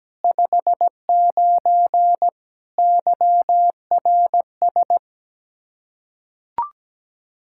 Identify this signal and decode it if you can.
Morse code, heard as a keyed tone, about 17 wpm, 700 Hz. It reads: 59 YRS
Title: Practice QSO - Exchange 3 - More Info - 17wpm